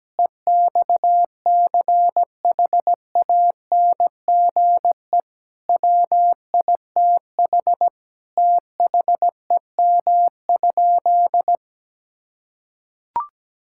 Morse code, heard as a keyed tone, about 17 wpm, 700 Hz. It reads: EXCHANGE WITH THEM?
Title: Sentences from Top 1000 Words - Review Part 6 - 17wpm